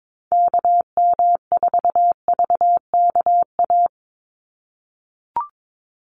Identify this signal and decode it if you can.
Morse code, heard as a keyed tone, about 22 wpm, 700 Hz. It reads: KM4VKA
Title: USA Call Signs 22wpm